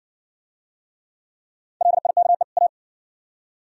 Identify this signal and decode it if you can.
Morse code, heard as a keyed tone, about 58 wpm, 700 Hz. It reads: LIKE A